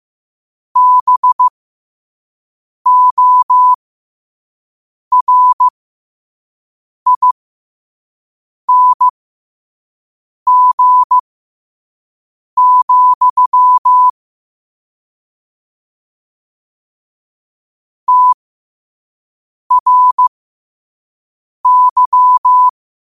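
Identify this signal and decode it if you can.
Morse code, heard as a keyed tone, about 15 wpm, spaced as for 5 wpm, 1000 Hz. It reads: BORING, TRY
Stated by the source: Morse quote-of-the-day practice at 5 wpm.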